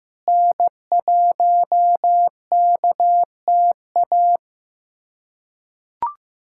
Morse code, heard as a keyed tone, about 15 wpm, 700 Hz. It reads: N1KTA
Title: USA Callsign DE Callsign 15wpm